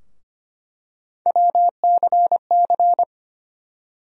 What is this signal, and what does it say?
Morse code, heard as a keyed tone, about 25 wpm, 700 Hz. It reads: WCC
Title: USA Call Sign Postfixes - 3 Letters 25wpm